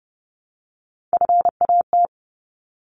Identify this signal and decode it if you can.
Morse code, heard as a keyed tone, about 30 wpm, 700 Hz. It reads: FAT
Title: Top 501-750 Words 30wpm